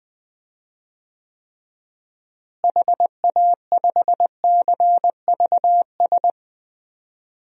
Morse code, heard as a keyed tone, about 21 wpm, 700 Hz. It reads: HA5CVS
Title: DXCC-100 Call Signs - 6 Characters - Repeated 2x - 20wpm